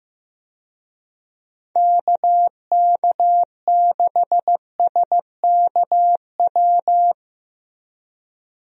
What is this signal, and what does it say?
Morse code, heard as a keyed tone, about 15 wpm, 700 Hz. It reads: KK6SKW
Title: USA Call Signs - 6 Characters - Repeated 3x - 15wpm